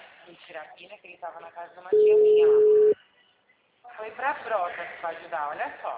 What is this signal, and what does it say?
Ringback tone (Europe, 425 Hz), with speech.